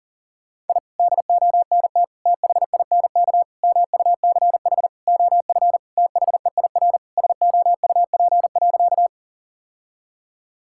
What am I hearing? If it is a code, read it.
Morse code, heard as a keyed tone, about 40 wpm, 700 Hz. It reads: I DONT THINK MUCH OF THEIR SOUP.
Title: Sentences from Top 1000 Words - Review Part 6 - 40wpm